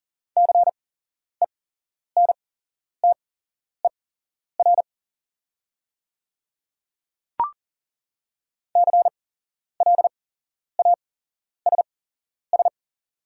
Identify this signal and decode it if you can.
Morse code, heard as a keyed tone, about 39 wpm, spaced as for 11 wpm, 700 Hz. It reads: CENTER CLASS
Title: 8x Character Spacing - Top 500 Words - 40wpm